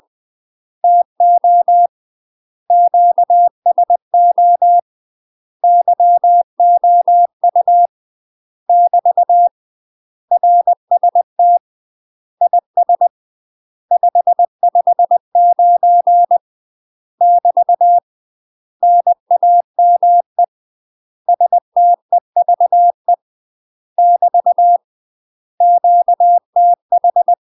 Morse code, heard as a keyed tone, about 20 wpm, 700 Hz. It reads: TO QSO YOU <BT> RST IS 559 <BT> NAME STEVE <BT> QTH
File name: UK MASTER QSOs TRAINING / 20WPM / UK MASTER QSOs 0005